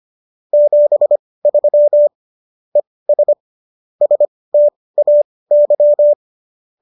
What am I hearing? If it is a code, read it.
Morse code, heard as a keyed tone, about 25 wpm, spaced as for 18 wpm, 590 Hz. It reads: 73 ES STAY